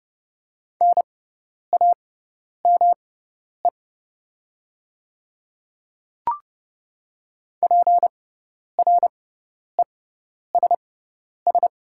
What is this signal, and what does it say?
Morse code, heard as a keyed tone, about 30 wpm, spaced as for 10 wpm, 700 Hz. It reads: NAME PRESS
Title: 6x Character Spacing - Top 500 Words - 30wpm